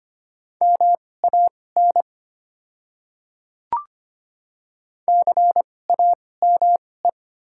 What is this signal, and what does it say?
Morse code, heard as a keyed tone, about 25 wpm, spaced as for 18 wpm, 700 Hz. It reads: MAN CAME
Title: 2x Character Spacing - Top 200 Words - 25wpm